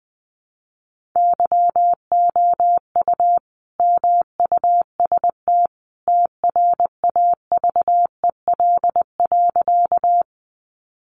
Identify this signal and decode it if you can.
Morse code, heard as a keyed tone, about 20 wpm, 700 Hz. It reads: YOU MUST TRAVEL.
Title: Sentences from Top 400 Words 20wpm